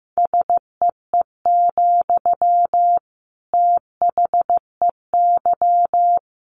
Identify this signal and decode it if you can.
Morse code, heard as a keyed tone, about 15 wpm, 700 Hz. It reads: SEE, THEY
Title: Sentences from Top 200 Words 15wpm